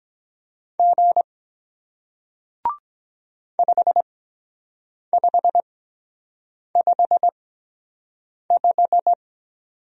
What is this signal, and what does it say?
Morse code, heard as a keyed tone, about 22 wpm, spaced as for 7 wpm, 700 Hz. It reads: G 5555